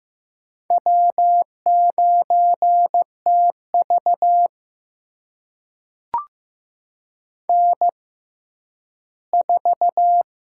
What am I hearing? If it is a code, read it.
Morse code, heard as a keyed tone, about 15 wpm, spaced as for 10 wpm, 700 Hz. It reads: W9TV N4
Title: USA Call Signs - 4 Characters - 6x Character Spacing - 15wpm